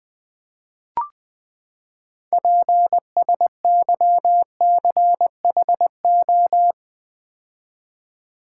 Morse code, heard as a keyed tone, about 20 wpm, 700 Hz. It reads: PSYCHO
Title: Build A Word - 6 Letters At A Time - 12-Letter Words - 20wpm